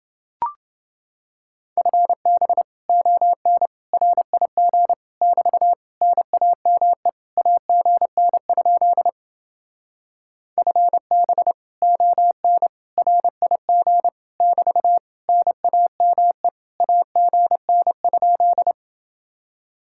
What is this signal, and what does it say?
Morse code, heard as a keyed tone, about 29 wpm, 700 Hz. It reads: FB ON RIG <BT> NAME AGN? FB ON RIG <BT> NAME AGN?